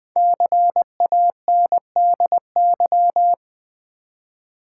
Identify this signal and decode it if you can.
Morse code, heard as a keyed tone, about 20 wpm, 700 Hz. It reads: CANDY
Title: Five Letter Words - 20wpm